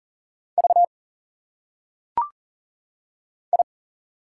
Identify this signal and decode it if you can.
Morse code, heard as a keyed tone, about 40 wpm, 700 Hz. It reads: V I
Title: Single Letters - Phonetic Pronunciation 40wpm